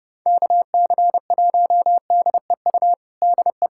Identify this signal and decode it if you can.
Morse code, heard as a keyed tone, about 30 wpm, 700 Hz. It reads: KC1DEU DE
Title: USA Callsign DE Callsign - 6 Characters - 30wpm